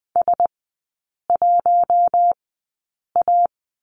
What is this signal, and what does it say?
Morse code, heard as a keyed tone, about 20 wpm, spaced as for 8 wpm, 700 Hz. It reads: S1A
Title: 152 Lesson 25e - Letter p - Three Characters 20wpm